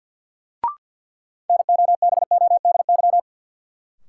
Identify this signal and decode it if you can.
Morse code, heard as a keyed tone, about 50 wpm, 700 Hz. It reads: NOBODY
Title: Sentences from Top 1600 Words - New Word Drill - 50wpm